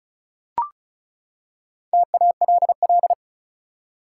Morse code, heard as a keyed tone, about 34 wpm, 700 Hz. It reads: TALL
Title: Four Letter Words - 35wpm